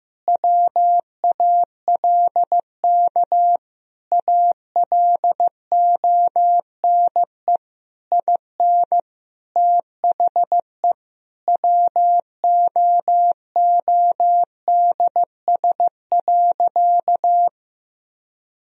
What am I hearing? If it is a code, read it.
Morse code, heard as a keyed tone, about 16 wpm, 700 Hz. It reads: WALK ALONE IN THE WOODS.
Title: Sentences from Top 1000 Words - Review Part 2 - 15wpm